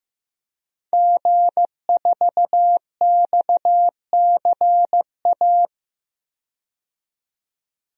Morse code, heard as a keyed tone, about 15 wpm, 700 Hz. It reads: G4XCA
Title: DXCC-100 Call Signs - 5 Characters - 15wpm